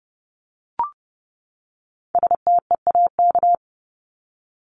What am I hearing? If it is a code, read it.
Morse code, heard as a keyed tone, about 30 wpm, 700 Hz. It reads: STEAK
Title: Top 1251-1500 Words 30wpm